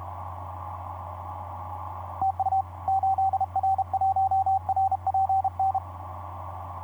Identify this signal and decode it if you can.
Morse code, heard as a keyed tone, about 32 wpm, 770 Hz. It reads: TU 8R1RPN
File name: 8r1rpn dl8mld 15m cw etc 27oct2010 1127